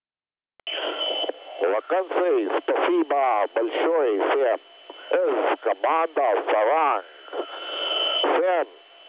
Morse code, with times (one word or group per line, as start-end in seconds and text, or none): none